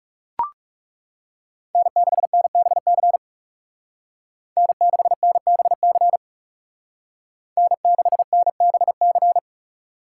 Morse code, heard as a keyed tone, about 35 wpm, 700 Hz. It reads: N6NBC N6NBC N6NBC
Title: Speed Racing - USA Call Signs 30wpm